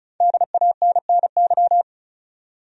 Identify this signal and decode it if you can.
Morse code, heard as a keyed tone, about 36 wpm, 700 Hz. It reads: DANNY
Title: USA Common Names - 35wpm